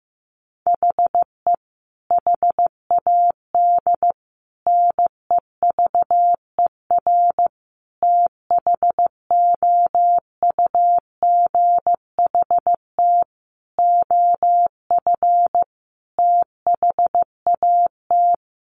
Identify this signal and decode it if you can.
Morse code, heard as a keyed tone, about 15 wpm, 700 Hz. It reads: HE HAD NEVER THOUGHT OF THAT
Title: Sentences from Top 1300 Words 15wpm